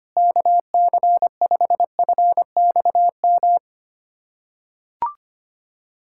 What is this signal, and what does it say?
Morse code, heard as a keyed tone, about 25 wpm, 700 Hz. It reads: KC5FXM
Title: USA Callsign DE Callsign 25wpm